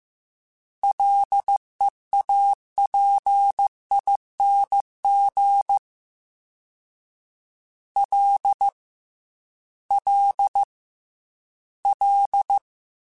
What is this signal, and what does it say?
Morse code, heard as a keyed tone, about 15 wpm, 800 Hz. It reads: LEAPING L L L
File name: Combinations / Combinations-mp3s / Combinations@13